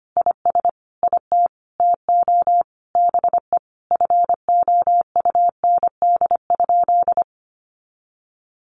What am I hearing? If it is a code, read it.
Morse code, heard as a keyed tone, about 25 wpm, 700 Hz. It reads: IS IT TO BE FOUND?
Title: Sentences from Top 300 Words 25wpm